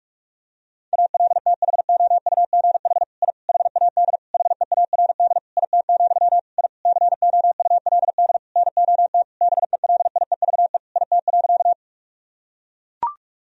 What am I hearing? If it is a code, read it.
Morse code, heard as a keyed tone, about 41 wpm, 700 Hz. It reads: ALTHOUGH I HAD HEARD IT, I COULD NOT BELIEVE IT.
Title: Sentences from Top 1000 Words - Review Part 3 - 45wpm